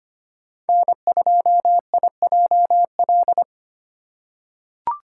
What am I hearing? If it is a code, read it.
Morse code, heard as a keyed tone, about 25 wpm, 700 Hz. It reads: N2IJL
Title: USA Call Signs - 5 Characters - 4x Character Spacing - 25wpm